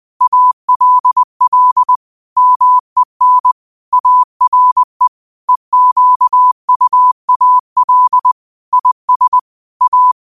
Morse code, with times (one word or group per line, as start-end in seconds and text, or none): 0.2-2.0 ALL
2.4-3.5 MEN
3.9-5.1 ARE
5.5-8.3 EQUAL
8.7-9.4 IS
9.8-10.1 A